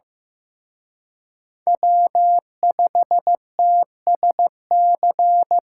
Morse code, heard as a keyed tone, about 18 wpm, 700 Hz. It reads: W5TSC
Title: Speed Racing - USA 1x3 Call Signs 15wpm